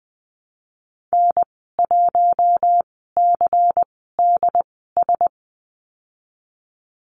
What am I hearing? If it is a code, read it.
Morse code, heard as a keyed tone, about 20 wpm, spaced as for 15 wpm, 700 Hz. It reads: N1CDS
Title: 200 Lesson 32g - Letter k - Call Signs 20wpm